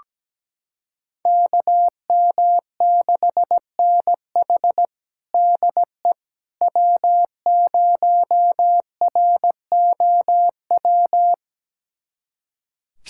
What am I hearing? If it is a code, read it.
Morse code, heard as a keyed tone, about 17 wpm, 700 Hz. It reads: KM6NH DE W0ROW
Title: USA Callsign DE Callsign - 5 Characters - 17wpm